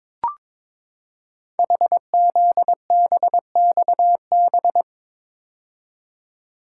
Morse code, heard as a keyed tone, about 22 wpm, 700 Hz. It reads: HZBXB
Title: Code Groups - 5 Letters - 22wpm